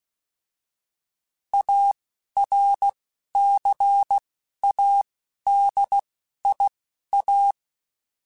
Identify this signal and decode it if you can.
Morse code, heard as a keyed tone, about 16 wpm, spaced as for 11 wpm, 800 Hz. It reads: ARCADIA
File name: fromSMs / FromSMs mp3s / Cities@10